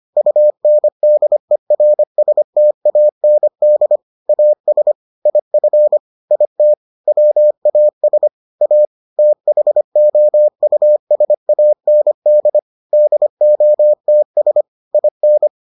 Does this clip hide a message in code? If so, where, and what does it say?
Yes: Morse code from 0.2 to 15.5 s, heard as a keyed tone, about 25 wpm, 600 Hz. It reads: UNDERSTAND AS IF IT WAS A THOUSAND DOTS IN